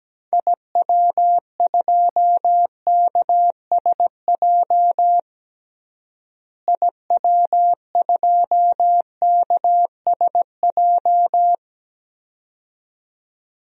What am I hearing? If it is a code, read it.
Morse code, heard as a keyed tone, about 17 wpm, 700 Hz. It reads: IW2KSJ IW2KSJ
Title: DXCC-100 Call Signs - 6 Characters - Repeated 2x - 17wpm